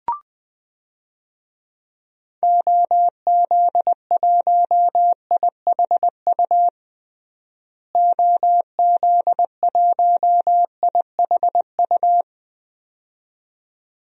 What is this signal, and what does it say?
Morse code, heard as a keyed tone, about 20 wpm, 700 Hz. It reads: OZ1IHU OZ1IHU
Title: DXCC-100 Call Signs - 6 Characters - Repeated 2x - 20wpm